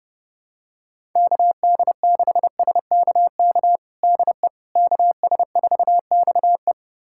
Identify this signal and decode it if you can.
Morse code, heard as a keyed tone, about 30 wpm, 700 Hz. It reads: KD6SKK DE KS4XE
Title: USA Callsign DE Callsign 30wpm